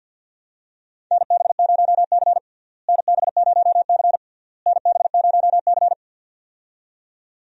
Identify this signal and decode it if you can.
Morse code, heard as a keyed tone, about 46 wpm, 700 Hz. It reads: NB0C NB0C NB0C
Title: USA Call Signs - 4 Characters - Repeated 3x - 50wpm